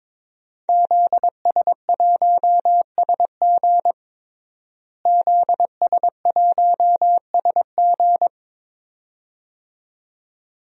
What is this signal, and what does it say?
Morse code, heard as a keyed tone, about 22 wpm, 700 Hz. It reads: ZS1SG ZS1SG
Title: DXCC-100 Call Signs - 5 Characters - Repeated 2x - 22wpm